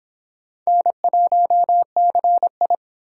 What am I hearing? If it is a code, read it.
Morse code, heard as a keyed tone, about 25 wpm, 700 Hz. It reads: N1CI